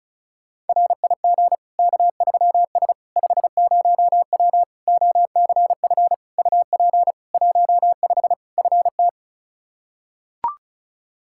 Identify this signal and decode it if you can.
Morse code, heard as a keyed tone, about 34 wpm, 700 Hz. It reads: RIG K3S 50W OCF UP 15 FT
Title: Practice QSO - Exchange 2 - Info - 35wpm